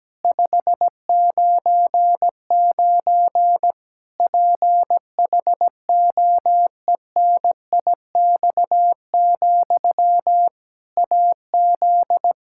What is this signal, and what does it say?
Morse code, heard as a keyed tone, about 17 wpm, 700 Hz. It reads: 599 PHOENIX, AZ
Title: K1USN SST QSO Practice 17wpm